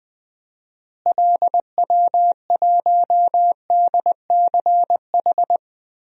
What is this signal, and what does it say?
Morse code, heard as a keyed tone, about 20 wpm, 700 Hz. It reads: LW1DCH
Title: DXCC-100 Call Signs - 6 Characters - Repeated 2x - 20wpm